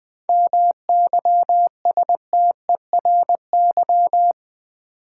Morse code, heard as a keyed tone, about 20 wpm, 700 Hz. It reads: MYSTERY
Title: 7 Letter Words - First 3 Letters Given - 20wpm